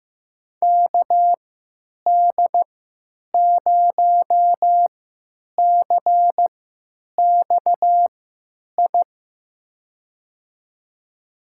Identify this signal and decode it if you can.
Morse code, heard as a keyed tone, about 15 wpm, spaced as for 9 wpm, 700 Hz. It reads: KD0CXI
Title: USA Call Signs - 6 Characters - 3x Character Spacing - 15wpm